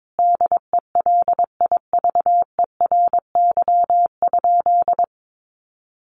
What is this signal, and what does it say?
Morse code, heard as a keyed tone, about 22 wpm, 700 Hz. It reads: DELIVERY?
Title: Sentences from Top 1500 Words 22wpm